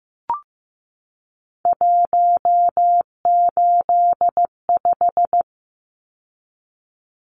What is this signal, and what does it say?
Morse code, heard as a keyed tone, about 15 wpm, 700 Hz. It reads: 185
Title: Numbers - 3-Digit 15wpm